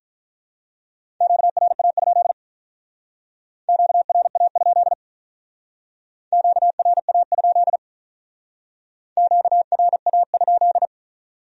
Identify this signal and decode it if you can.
Morse code, heard as a keyed tone, about 43 wpm, 700 Hz. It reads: QRA? QRA? QRA? QRA?
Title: Speed-Racing Q-Codes - 35wpm